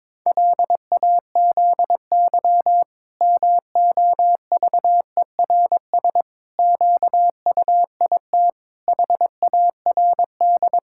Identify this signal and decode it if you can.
Morse code, heard as a keyed tone, about 22 wpm, 700 Hz. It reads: LAZY MOVERS QUIT HARD
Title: Pangram Challenge v2 - 22wpm